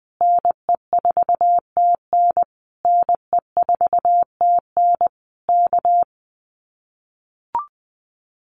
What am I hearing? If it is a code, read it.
Morse code, heard as a keyed tone, about 20 wpm, 700 Hz. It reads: NE4TN NE4TN K
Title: Common QSO Phrases Encore 20wpm